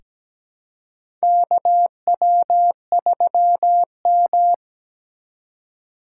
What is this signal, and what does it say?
Morse code, heard as a keyed tone, about 17 wpm, 700 Hz. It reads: KW3M
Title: USA Call Signs - 4 Characters - 4x Character Spacing - 17wpm